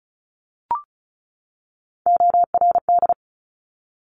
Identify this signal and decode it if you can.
Morse code, heard as a keyed tone, about 36 wpm, 700 Hz. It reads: ORD
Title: Top Three Letter Combinations in Words 35wpm